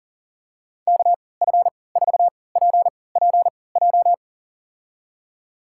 Morse code, heard as a keyed tone, about 37 wpm, spaced as for 22 wpm, 700 Hz. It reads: KF4PPJ
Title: USA Call Signs - 6 Characters - 3x Character Spacing - 40wpm